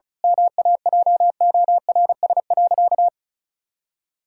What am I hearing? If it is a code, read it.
Morse code, heard as a keyed tone, about 36 wpm, 700 Hz. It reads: MAJORS.
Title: Pangram Challenge v2 - 35wpm